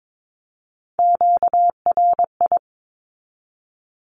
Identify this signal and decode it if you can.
Morse code, heard as a keyed tone, about 22 wpm, 700 Hz. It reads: QRI
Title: Q-Codes 22wpm